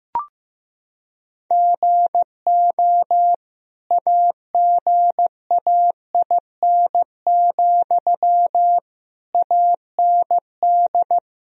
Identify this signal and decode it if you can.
Morse code, heard as a keyed tone, about 15 wpm, 700 Hz. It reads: GO AGAIN, AND
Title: Sentences from Top 1000 Words - Review Part 4 - 15wpm